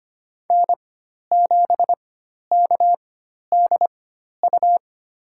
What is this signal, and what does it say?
Morse code, heard as a keyed tone, about 25 wpm, spaced as for 12 wpm, 700 Hz. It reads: N7KDU
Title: USA Call Signs - 5 Characters - 4x Character Spacing - 25wpm